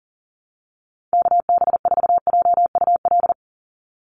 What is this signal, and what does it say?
Morse code, heard as a keyed tone, about 42 wpm, 700 Hz. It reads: KB4JUL
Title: USA Call Signs 40wpm